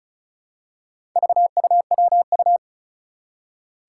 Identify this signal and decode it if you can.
Morse code, heard as a keyed tone, about 36 wpm, 700 Hz. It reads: VUWU
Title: Four Challenging Letter-Number Combinations 35wpm